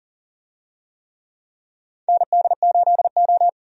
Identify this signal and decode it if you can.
Morse code, heard as a keyed tone, about 40 wpm, 700 Hz. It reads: ND8O
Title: USA Call Signs - 4 Characters - 40wpm